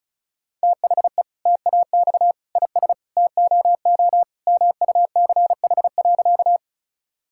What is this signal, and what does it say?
Morse code, heard as a keyed tone, about 36 wpm, 700 Hz. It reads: THE TAX IS TOO MUCH.
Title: Sentences from Top 1000 Words - Review Part 5 - 35wpm